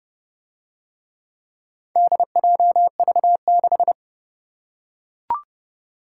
Morse code, heard as a keyed tone, about 27 wpm, 700 Hz. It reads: DJV6